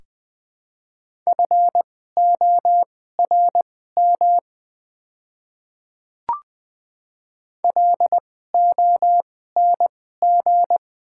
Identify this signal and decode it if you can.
Morse code, heard as a keyed tone, about 20 wpm, spaced as for 14 wpm, 700 Hz. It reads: FORM LONG
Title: 2x Character Spacing - Top 200 Words - 20wpm